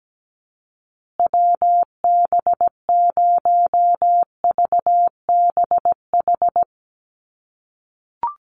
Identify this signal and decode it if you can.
Morse code, heard as a keyed tone, about 17 wpm, 700 Hz. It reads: WB0VBH
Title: USA Call Signs Repeated 3x 17wpm